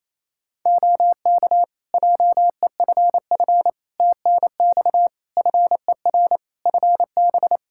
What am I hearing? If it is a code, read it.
Morse code, heard as a keyed tone, about 28 wpm, 700 Hz. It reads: OK JEFF TNX FER FB
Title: Practice QSO - Exchange 4 - Ending - 28wpm